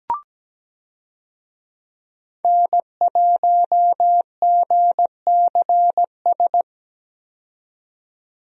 Morse code, heard as a keyed tone, about 17 wpm, 700 Hz. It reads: N1GCS
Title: USA Call Signs - 5 Characters - 17wpm